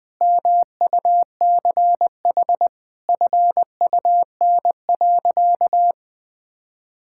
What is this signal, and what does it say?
Morse code, heard as a keyed tone, about 20 wpm, 700 Hz. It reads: MUCH FUN.
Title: Sentences from Top 1600 Words 20wpm